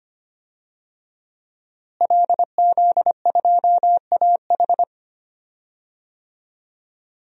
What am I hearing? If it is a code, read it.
Morse code, heard as a keyed tone, about 25 wpm, 700 Hz. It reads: LZ2AH
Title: DXCC-100 Call Signs - 5 Characters - Repeated 2x - 25wpm